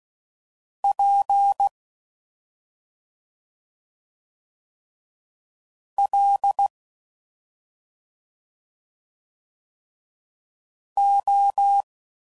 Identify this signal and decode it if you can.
Morse code, heard as a keyed tone, about 16 wpm, 800 Hz. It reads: P L O